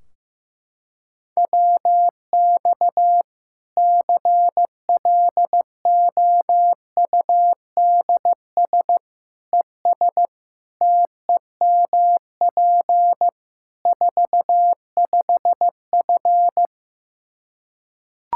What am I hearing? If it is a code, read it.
Morse code, heard as a keyed tone, about 15 wpm, 700 Hz. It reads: WX CLOUDS ES TEMP 45F
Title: Practice QSO - Exchange 2 - Info - 15wpm